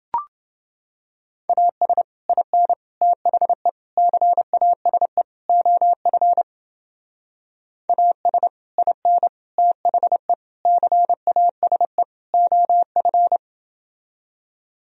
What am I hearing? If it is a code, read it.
Morse code, heard as a keyed tone, about 29 wpm, 700 Hz. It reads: AS IN THE CASE OF AS IN THE CASE OF